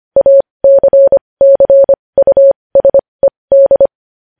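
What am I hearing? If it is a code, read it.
Morse code, heard as a keyed tone, about 25 wpm, spaced as for 20 wpm, 550 Hz. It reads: ACCUSED
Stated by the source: Morse practice sending news headlines at 20 wpm.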